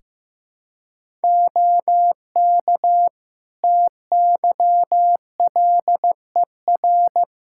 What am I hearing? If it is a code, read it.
Morse code, heard as a keyed tone, about 15 wpm, 700 Hz. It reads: OK TYLER
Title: Practice QSO - Exchange 3 - More Info - 15wpm